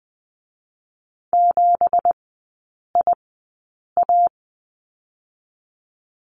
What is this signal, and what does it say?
Morse code, heard as a keyed tone, about 20 wpm, spaced as for 8 wpm, 700 Hz. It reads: 7IA
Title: 218 Lesson 35e - Number 0 - Three Characters 20wpm